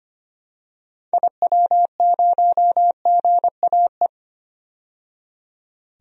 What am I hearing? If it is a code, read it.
Morse code, heard as a keyed tone, about 25 wpm, 700 Hz. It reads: IW0GAE
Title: DXCC-100 Call Signs - 6 Characters - 25wpm